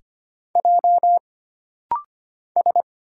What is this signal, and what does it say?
Morse code, heard as a keyed tone, about 25 wpm, 700 Hz. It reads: J S
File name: Single Character (Letter:Number:Punctuation:Prosign) - Rapid-Fire - 25wpm